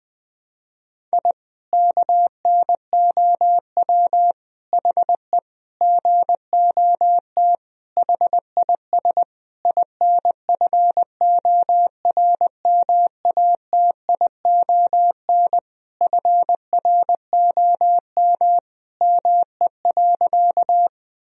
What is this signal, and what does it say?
Morse code, heard as a keyed tone, about 20 wpm, 700 Hz. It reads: I KNOW HE GOT HIS INFORMATION FROM ME.
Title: Sentences from Top 1000 Words - Review Part 3 - 20wpm